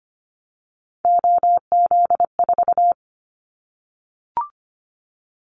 Morse code, heard as a keyed tone, about 25 wpm, 700 Hz. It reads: OZ4
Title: Callsign Prefixes for Top 100 DXCC - 25wpm